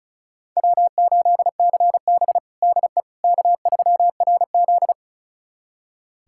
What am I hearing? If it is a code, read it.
Morse code, heard as a keyed tone, about 36 wpm, 700 Hz. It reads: W8CB DE K3RZ
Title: USA Callsign DE Callsign - 4 Characters - 35wpm